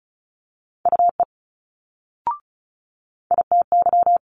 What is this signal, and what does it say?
Morse code, heard as a keyed tone, about 33 wpm, 700 Hz. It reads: UE ITY
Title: Multi-letter Phonemes Quick Fire 35wpm